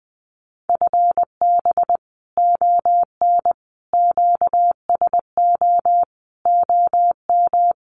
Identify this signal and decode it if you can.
Morse code, heard as a keyed tone, about 20 wpm, 700 Hz. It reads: FB ON QSO OM
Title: Common QSO Phrases 20wpm